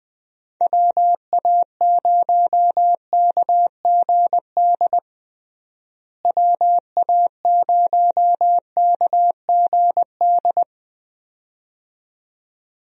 Morse code, heard as a keyed tone, about 20 wpm, 700 Hz. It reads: WA0KGD WA0KGD
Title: USA Call Signs - 6 Characters - Repeated 2x - 20wpm